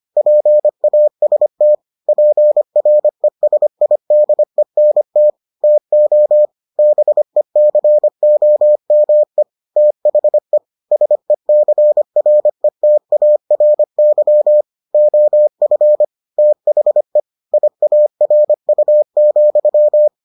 Morse code, heard as a keyed tone, about 25 wpm, 600 Hz. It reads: PAST PRESIDENT TO BECOME THE SECRETARY OF THE IARU,